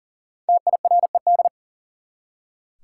Morse code, heard as a keyed tone, about 40 wpm, 700 Hz. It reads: TIRED
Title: Five Letter Words - 40wpm